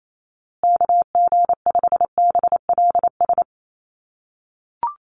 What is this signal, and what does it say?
Morse code, heard as a keyed tone, about 28 wpm, 700 Hz. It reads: KG5BLS
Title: USA Call Signs Repeated 2x 28wpm